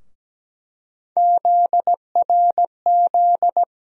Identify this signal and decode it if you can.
Morse code, heard as a keyed tone, about 17 wpm, 700 Hz. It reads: ZRZ
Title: USA Call Sign Postfixes - 3 Letters 17wpm